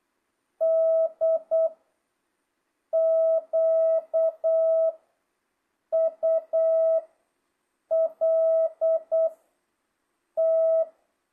Morse code, heard as a keyed tone, about 8 wpm, 660 Hz. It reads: D Q U L T
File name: jh8cbh / 2025cw